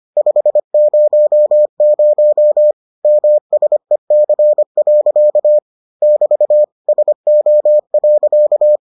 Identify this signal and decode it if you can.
Morse code, heard as a keyed tone, about 25 wpm, 600 Hz. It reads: 500 MSEC. <BT> SO.